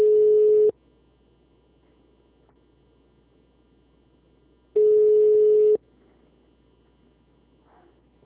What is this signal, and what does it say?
Ringback tone (Europe, 425 Hz).